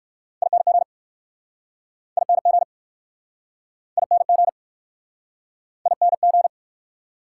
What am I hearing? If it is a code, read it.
Morse code, heard as a keyed tone, about 60 wpm, 690 Hz. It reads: ING ING ING ING